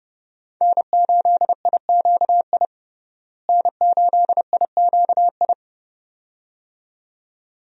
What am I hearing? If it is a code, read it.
Morse code, heard as a keyed tone, about 30 wpm, 700 Hz. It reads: N8IQI N8IQI
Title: USA Call Signs - 5 Characters - Repeated 2x - 30wpm